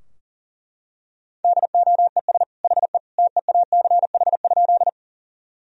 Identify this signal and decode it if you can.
Morse code, heard as a keyed tone, about 40 wpm, 700 Hz. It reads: DOES HE TEACH?
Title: Sentences from Top 1000 Words - Review Part 2 - 40wpm